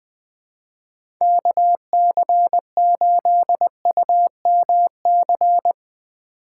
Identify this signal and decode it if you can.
Morse code, heard as a keyed tone, about 20 wpm, 700 Hz. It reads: KC8UMC
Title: USA Call Signs - 6 Characters - 2x Character Spacing - 20wpm